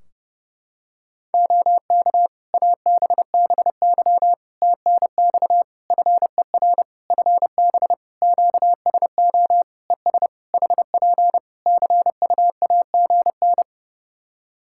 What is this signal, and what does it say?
Morse code, heard as a keyed tone, about 30 wpm, 700 Hz. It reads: OK ABBY TNX FER FB QSO ES HP CUAGN
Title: Practice QSO - Exchange 4 - Ending - 30wpm